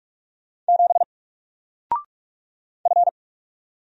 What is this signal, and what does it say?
Morse code, heard as a keyed tone, about 44 wpm, 700 Hz. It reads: 7 F
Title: Single Letter-Number Alt 45wpm